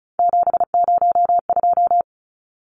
Morse code, heard as a keyed tone, about 34 wpm, 700 Hz. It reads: 702
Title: Numbers - 3-Digit 35wpm